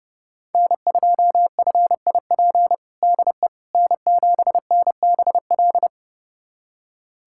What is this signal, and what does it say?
Morse code, heard as a keyed tone, about 32 wpm, 700 Hz. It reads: N2FIP DE N7NBL
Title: USA Callsign DE Callsign - 5 Characters - 30wpm